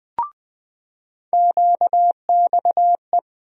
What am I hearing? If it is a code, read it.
Morse code, heard as a keyed tone, about 20 wpm, 700 Hz. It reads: QXE
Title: USA Call Sign Postfixes - 3 Letters 20wpm